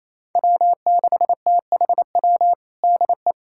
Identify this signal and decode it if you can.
Morse code, heard as a keyed tone, about 28 wpm, 700 Hz. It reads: W6THW DE
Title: USA Callsign DE Callsign - 5 Characters - 28wpm